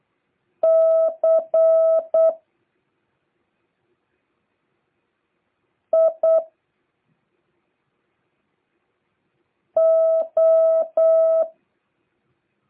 Morse code, heard as a keyed tone, about 8 wpm, 650 Hz. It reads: C I O